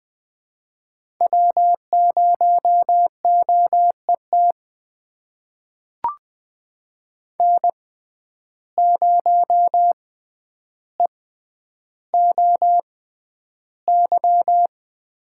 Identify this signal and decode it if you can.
Morse code, heard as a keyed tone, about 20 wpm, spaced as for 10 wpm, 700 Hz. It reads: W0OET N0EOY
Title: USA Call Signs - 5 Characters - 6x Character Spacing - 20wpm